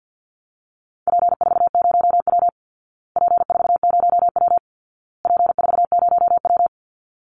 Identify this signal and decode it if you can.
Morse code, heard as a keyed tone, about 48 wpm, 700 Hz. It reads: P40W P40W P40W
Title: Call Signs Repeated 2x - 50wpm